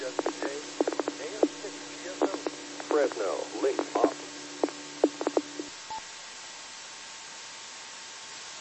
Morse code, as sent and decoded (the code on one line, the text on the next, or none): none